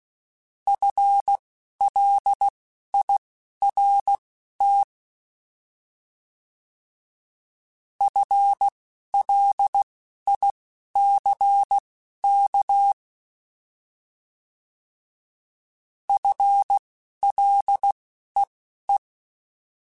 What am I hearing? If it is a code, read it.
Morse code, heard as a keyed tone, about 16 wpm, spaced as for 11 wpm, 800 Hz. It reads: FLIRT FLICK FLEE
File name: Alphabetical / Alphabetical mp3s / Fs @10